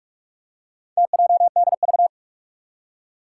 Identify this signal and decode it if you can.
Morse code, heard as a keyed tone, about 46 wpm, 700 Hz. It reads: TJDV
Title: Four Challenging Letter-Number Combinations 45wpm